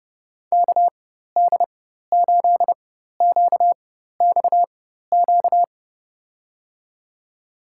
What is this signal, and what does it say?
Morse code, heard as a keyed tone, about 31 wpm, spaced as for 14 wpm, 700 Hz. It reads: KD8QXQ
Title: USA Call Signs - 6 Characters - 4x Character Spacing - 30wpm